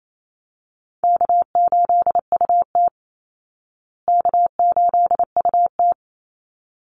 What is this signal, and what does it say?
Morse code, heard as a keyed tone, about 28 wpm, 700 Hz. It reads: K8UT K8UT
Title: Call Signs Repeated 3x - 28wpm